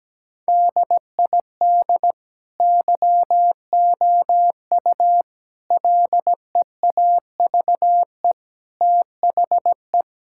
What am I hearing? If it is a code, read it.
Morse code, heard as a keyed tone, about 17 wpm, 700 Hz. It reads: DID YOU LEAVE THE
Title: Sentences from Top 1600 Words 17wpm